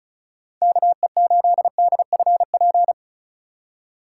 Morse code, heard as a keyed tone, about 34 wpm, 700 Hz. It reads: KE8DFP
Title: USA Call Signs - 4x Character Spacing - 35wpm